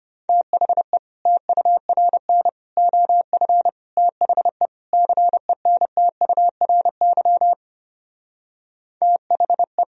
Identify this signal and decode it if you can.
Morse code, heard as a keyed tone, about 33 wpm, 700 Hz. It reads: THE TURN OF THE CENTURY THE